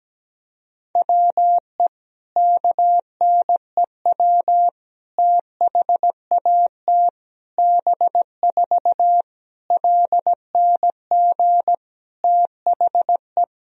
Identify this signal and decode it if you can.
Morse code, heard as a keyed tone, about 17 wpm, 700 Hz. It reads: WE KNEW THAT B4 LNG THE
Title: Sentences from Top 400 Words and Abbreviations 17wpm